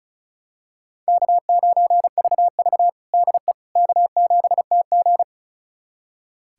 Morse code, heard as a keyed tone, about 36 wpm, 700 Hz. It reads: K9VV DE K7TG
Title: USA Callsign DE Callsign - 4 Characters - 35wpm